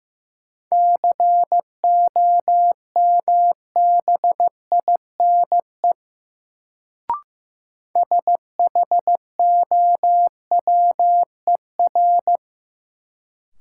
Morse code, heard as a keyed tone, about 15 wpm, 700 Hz. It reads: COMBINE SHOWER
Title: Sentences from Top 1600 Words - New Word Drill - 15wpm